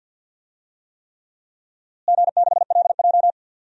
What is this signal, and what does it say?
Morse code, heard as a keyed tone, about 47 wpm, 700 Hz. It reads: G6LJ
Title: DXCC-100 Call Signs - 4 Characters - Repeated 3x - 50wpm